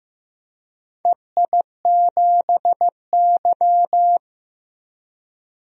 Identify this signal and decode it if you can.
Morse code, heard as a keyed tone, about 15 wpm, 700 Hz. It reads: EI7Y
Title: DXCC-100 Call Signs - 4 Characters - 15wpm